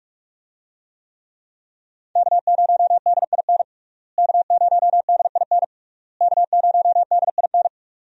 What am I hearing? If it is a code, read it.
Morse code, heard as a keyed tone, about 41 wpm, 700 Hz. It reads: K0DIN K0DIN K0DIN
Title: USA Call Signs - 5 Characters - Repeated 3x - 45wpm